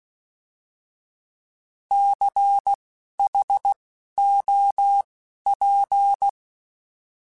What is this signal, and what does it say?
Morse code, heard as a keyed tone, about 16 wpm, spaced as for 11 wpm, 800 Hz. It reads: CHOP
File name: Alphabetical / Alphabetical mp3s / Cs @10